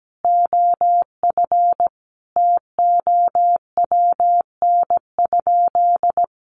Morse code, heard as a keyed tone, about 17 wpm, 700 Hz. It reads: OF TOWN?
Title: Sentences from Top 1300 Words 17wpm